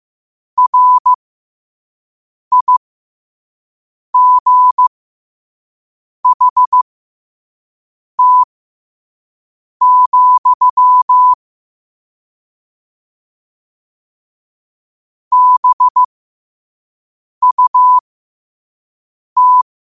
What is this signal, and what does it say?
Morse code, heard as a keyed tone, about 15 wpm, spaced as for 5 wpm, 1000 Hz. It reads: RIGHT, BUT
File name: qotdfiles / 5wpm